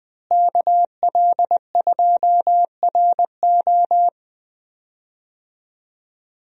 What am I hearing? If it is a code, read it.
Morse code, heard as a keyed tone, about 20 wpm, 700 Hz. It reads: KL2RO
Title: USA Call Signs - 5 Characters - Repeated 3x - 20wpm